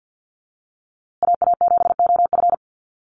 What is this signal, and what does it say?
Morse code, heard as a keyed tone, about 50 wpm, 700 Hz. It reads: AA7OF
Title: USA Call Signs Repeated 3x 50wpm